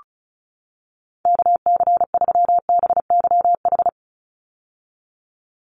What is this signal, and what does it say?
Morse code, heard as a keyed tone, about 36 wpm, 700 Hz. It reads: KC3BYH
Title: USA Call Signs 35wpm